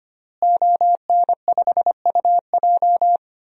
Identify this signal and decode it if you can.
Morse code, heard as a keyed tone, about 25 wpm, 700 Hz. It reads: ON5UJ
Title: DXCC-100 Call Signs - 5 Characters - Repeated 2x - 25wpm